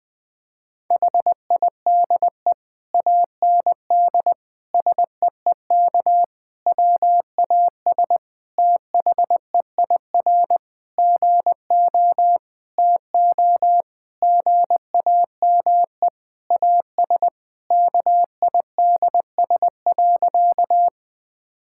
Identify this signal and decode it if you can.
Morse code, heard as a keyed tone, about 20 wpm, 700 Hz. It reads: HIDE AND SEEK WAS THEIR GO TO GAME AS KIDS.
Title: Difficult Sentences with Hints - 20wpm